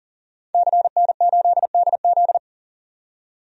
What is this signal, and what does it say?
Morse code, heard as a keyed tone, about 40 wpm, 700 Hz. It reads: CN8DZ
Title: DXCC-100 Call Signs - 5 Characters - 40wpm